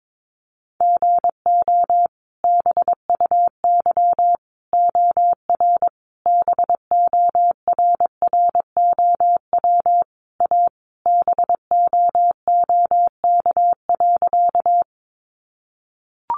Morse code, heard as a keyed tone, about 22 wpm, 700 Hz. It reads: GO BUY OR BORROW A BOOK.
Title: Sentences from Top 1500 Words 22wpm